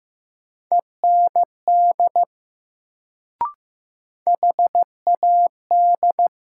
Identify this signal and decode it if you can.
Morse code, heard as a keyed tone, about 15 wpm, 700 Hz. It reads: END HAD
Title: Sets of 2 Words - New Word Drill - 15wpm